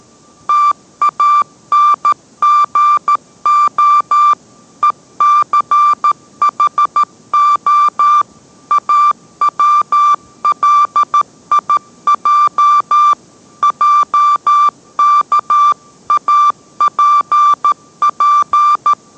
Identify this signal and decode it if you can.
Morse code, heard as a keyed tone, about 13 wpm, 1180 Hz. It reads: TANGO ECHO AWLIJ JKAPP